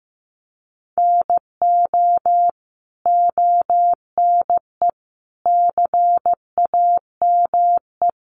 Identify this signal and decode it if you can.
Morse code, heard as a keyed tone, about 15 wpm, 700 Hz. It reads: NO ONE CAME
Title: Sentences from Top 1300 Words 15wpm